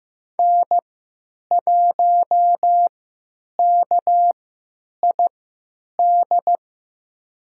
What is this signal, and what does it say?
Morse code, heard as a keyed tone, about 15 wpm, spaced as for 9 wpm, 700 Hz. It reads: N1KID
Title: USA Call Signs - 5 Characters - 3x Character Spacing - 15wpm